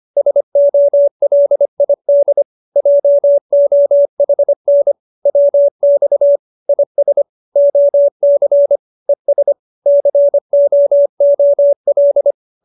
Morse code, heard as a keyed tone, about 25 wpm, 570 Hz. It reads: SOLID JOHN WX IS OC ES COOL